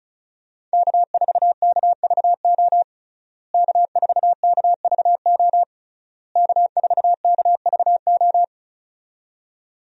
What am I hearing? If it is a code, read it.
Morse code, heard as a keyed tone, about 36 wpm, 700 Hz. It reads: K4KVO K4KVO K4KVO
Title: USA Call Signs - 5 Characters - Repeated 3x - 35wpm